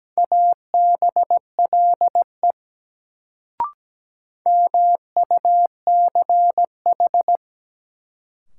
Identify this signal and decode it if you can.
Morse code, heard as a keyed tone, about 17 wpm, 700 Hz. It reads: ABLE MUCH
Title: Sets of 2 Words - New Word Drill - 17wpm